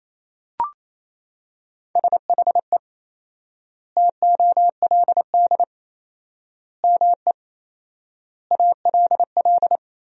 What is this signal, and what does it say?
Morse code, heard as a keyed tone, about 28 wpm, 700 Hz. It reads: SHE TOLD ME ALL
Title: 4x Word-Spacing - Sentences from Top 400 Words - 28wpm